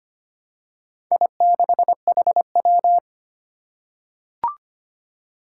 Morse code, heard as a keyed tone, about 25 wpm, 700 Hz. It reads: I6HW
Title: Four Challenging Letter-Number Combinations 25wpm